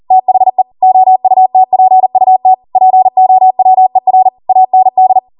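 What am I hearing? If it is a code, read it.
Morse code, heard as a keyed tone, about 40 wpm, 750 Hz. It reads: THE OUTPUT POWER AND
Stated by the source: Code practice at 40 wpm.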